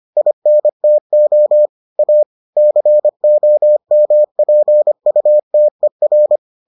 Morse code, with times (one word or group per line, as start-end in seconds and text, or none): 0.2-1.7 INTO
2.0-2.2 A
2.6-6.4 COMPUTER